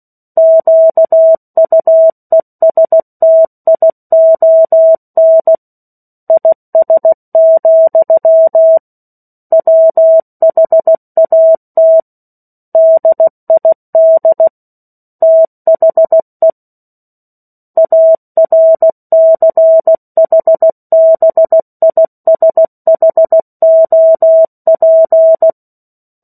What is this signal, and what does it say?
Morse code, heard as a keyed tone, about 16 wpm, 650 Hz. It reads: QUESTION IS, WHAT DID THE ARCHBISHOP
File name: 03-Chapter-03 16 WPM